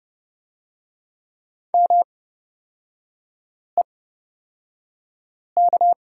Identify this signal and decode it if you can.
Morse code, heard as a keyed tone, about 30 wpm, spaced as for 5 wpm, 700 Hz. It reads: MEK
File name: Lesson 36e - BT - Three Characters 30wpm